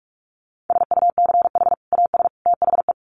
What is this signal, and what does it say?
Morse code, heard as a keyed tone, about 44 wpm, 700 Hz. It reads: SUCH AS THE
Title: Sets of 3 Words 45wpm